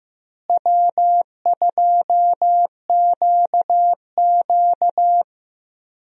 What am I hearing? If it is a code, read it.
Morse code, heard as a keyed tone, about 15 wpm, 700 Hz. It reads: W2QQ
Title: USA Call Signs - 4 Characters - Repeated 3x - 15wpm